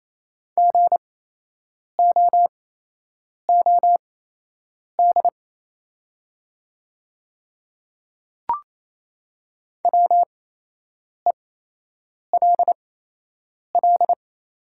Morse code, heard as a keyed tone, about 28 wpm, spaced as for 8 wpm, 690 Hz. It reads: GOOD WELL
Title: 8x Character Spacing - Top 200 Words - 28wpm